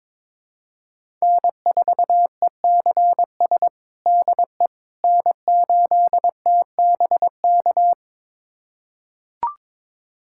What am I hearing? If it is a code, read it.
Morse code, heard as a keyed tone, about 21 wpm, 700 Hz. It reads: N4ECS DE N8TBK
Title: USA Callsign DE Callsign - 5 Characters - 22wpm